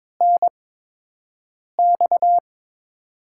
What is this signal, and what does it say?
Morse code, heard as a keyed tone, about 22 wpm, 700 Hz. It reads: N X
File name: Top 20 QSO Elements - 8x Character Spacing - 22wpm